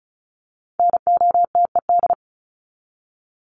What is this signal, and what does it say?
Morse code, heard as a keyed tone, about 34 wpm, 700 Hz. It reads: NOTED
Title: ED Suffix - Top 100 Words with Base Words 35wpm